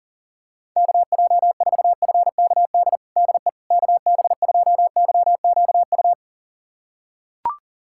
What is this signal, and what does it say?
Morse code, heard as a keyed tone, about 40 wpm, 700 Hz. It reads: KJ4FKD DE KB2YQU
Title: USA Callsign DE Callsign 40wpm